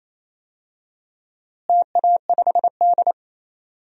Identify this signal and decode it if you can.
Morse code, heard as a keyed tone, about 27 wpm, 700 Hz. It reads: TA5D
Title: DXCC-100 Call Signs - 4 Characters - Repeated 2x - 28wpm